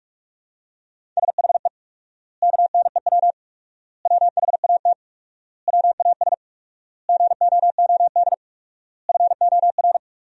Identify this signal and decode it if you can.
Morse code, heard as a keyed tone, about 46 wpm, 700 Hz. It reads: SHE KNEW WHAT WAS GOOD FOR
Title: 4x Word-Spacing - Sentences from Top 400 Words - 45wpm